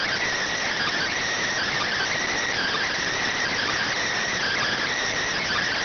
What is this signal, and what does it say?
Emergency Alert System (SAME) tones, heard as harsh buzzing bursts. It is heard from 0.7 to 4.8 s.